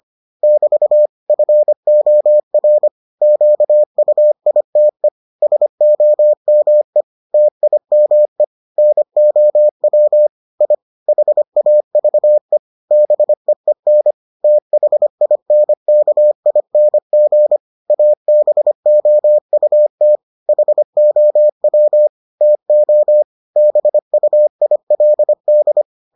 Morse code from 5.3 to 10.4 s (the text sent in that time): SOME TIME NOW